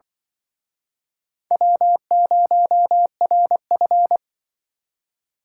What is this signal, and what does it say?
Morse code, heard as a keyed tone, about 23 wpm, 700 Hz. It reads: W0RF
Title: Speed Racing - USA 1x2 Call Signs 20wpm